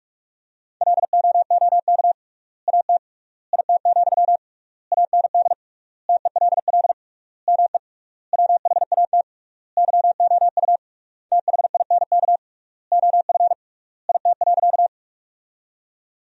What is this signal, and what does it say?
Morse code, heard as a keyed tone, about 44 wpm, 700 Hz. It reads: LOOK AT IT, AND TELL ME WHAT YOU THINK OF IT.
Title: 3x Word-Spacing - Sentences from Top 200 Words - 45wpm